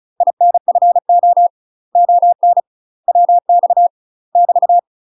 Morse code, heard as a keyed tone, about 37 wpm, 700 Hz. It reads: INFO ON WX <BT>
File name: UK MASTER QSOs TRAINING / 35WPM / UK MASTER QSOs 0008